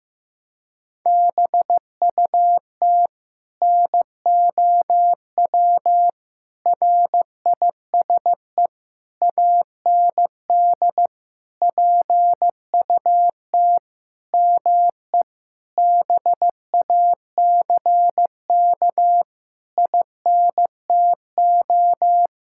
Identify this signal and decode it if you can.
Morse code, heard as a keyed tone, about 15 wpm, 700 Hz. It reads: BUT NOW RISE AND PUT ME BACK INTO
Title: Sentences from Top 1000 Words - Review Part 6 - 15wpm